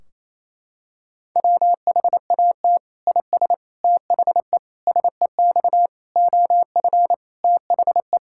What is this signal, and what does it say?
Morse code, heard as a keyed tone, about 28 wpm, 700 Hz. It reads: WHAT IS THE SEX OF THE
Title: Sentences from Top 1600 Words 28wpm